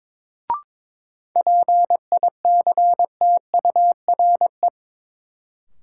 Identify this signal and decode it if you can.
Morse code, heard as a keyed tone, about 21 wpm, 700 Hz. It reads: PICTURE
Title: Sentences from Top 500 Words - New Word Drill - 22wpm